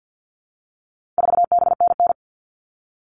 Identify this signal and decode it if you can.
Morse code, heard as a keyed tone, about 50 wpm, 700 Hz. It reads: 4BNN
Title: All Cut Numbers - 4-Digit 50wpm